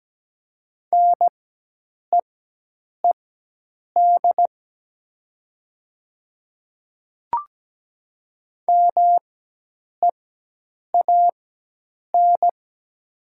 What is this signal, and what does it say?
Morse code, heard as a keyed tone, about 17 wpm, spaced as for 8 wpm, 700 Hz. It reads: NEED MEAN
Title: 4x Character Spacing - Top 200 Words - 17wpm